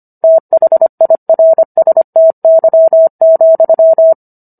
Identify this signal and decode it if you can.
Morse code, heard as a keyed tone, about 25 wpm, 650 Hz. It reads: THIRSTY,